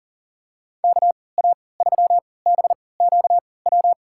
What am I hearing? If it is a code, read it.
Morse code, heard as a keyed tone, about 40 wpm, spaced as for 23 wpm, 700 Hz. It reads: KA3BQW
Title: USA Call Signs - 6 Characters - 3x Character Spacing - 40wpm